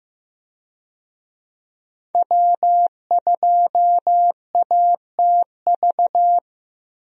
Morse code, heard as a keyed tone, about 15 wpm, 700 Hz. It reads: W2ATV
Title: USA Call Signs - 5 Characters - 15wpm